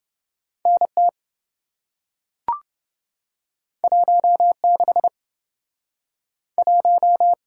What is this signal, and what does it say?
Morse code, heard as a keyed tone, about 27 wpm, 700 Hz. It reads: NT 16 1